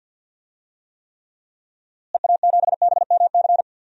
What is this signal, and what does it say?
Morse code, heard as a keyed tone, about 47 wpm, 700 Hz. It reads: EA7BMC
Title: DXCC-100 Call Signs - 6 Characters - Repeated 2x - 50wpm